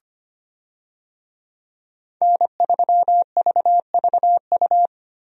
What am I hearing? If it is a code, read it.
Morse code, heard as a keyed tone, about 25 wpm, 700 Hz. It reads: N3VVU
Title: USA Call Signs - 5 Characters - Repeated 3x - 25wpm